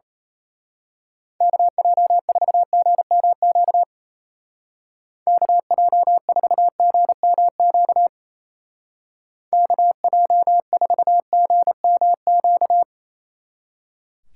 Morse code, heard as a keyed tone, about 32 wpm, 700 Hz. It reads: KJ4GMQ KJ4GMQ KJ4GMQ
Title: Speed Racing - USA 2x3 Call Signs 28wpm